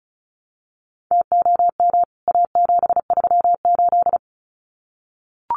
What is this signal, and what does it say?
Morse code, heard as a keyed tone, about 34 wpm, 700 Hz. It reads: TOM A738
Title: CWT Exchange 35wpm